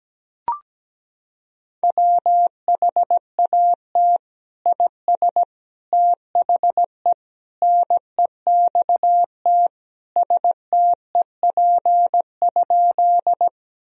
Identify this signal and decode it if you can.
Morse code, heard as a keyed tone, about 17 wpm, 700 Hz. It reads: WHAT IS THE NEXT STEP?
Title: Sentences from Top 1000 Words - Review Part 2 - 17wpm